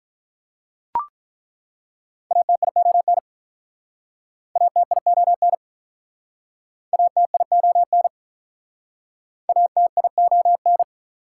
Morse code, heard as a keyed tone, about 49 wpm, 700 Hz. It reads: ATION ATION ATION ATION